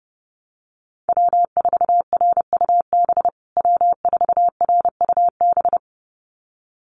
Morse code, heard as a keyed tone, about 30 wpm, 700 Hz. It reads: W4RUB W4RUB
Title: USA Call Signs Repeated 2x 30wpm